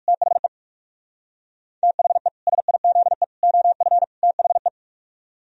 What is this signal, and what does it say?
Morse code, heard as a keyed tone, about 58 wpm, 700 Hz. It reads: THE THE SIZE OF THE